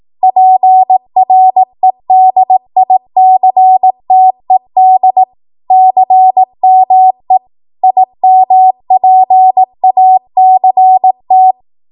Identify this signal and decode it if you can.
Morse code, heard as a keyed tone, about 18 wpm, 750 Hz. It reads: PREDICTED CME IMPACT
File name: Archive / 18 WPM / 18WPM